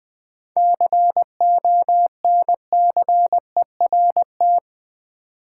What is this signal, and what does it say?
Morse code, heard as a keyed tone, about 20 wpm, 700 Hz. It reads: CONCERT
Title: Seven Letter Words - 20wpm